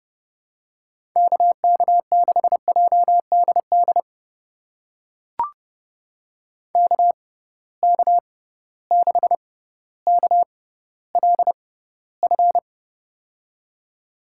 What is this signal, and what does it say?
Morse code, heard as a keyed tone, about 30 wpm, spaced as for 15 wpm, 700 Hz. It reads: KK6JDD KK6KLF
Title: USA Call Signs - 6 Characters - 6x Character Spacing - 30wpm